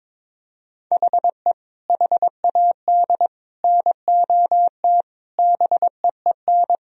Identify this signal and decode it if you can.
Morse code, heard as a keyed tone, about 22 wpm, 700 Hz. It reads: HE HAD NOT BEEN
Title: Sentences from Top 1000 Words - Review Part 4 - 22wpm